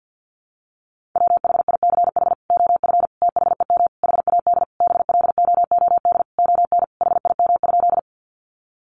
Morse code, heard as a keyed tone, about 49 wpm, 700 Hz. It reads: WHICH OF THEM HAD BLOOD ON HIM?
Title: Sentences from Top 1100 Words 50wpm